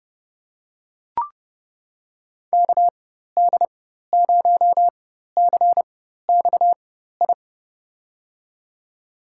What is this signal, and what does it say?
Morse code, heard as a keyed tone, about 30 wpm, spaced as for 14 wpm, 700 Hz. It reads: KD0CXI
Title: USA Call Signs - 6 Characters - 4x Character Spacing - 30wpm